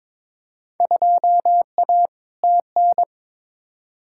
Morse code, heard as a keyed tone, about 22 wpm, 700 Hz. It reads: 2A TN
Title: Field Day Exchanges - 22wpm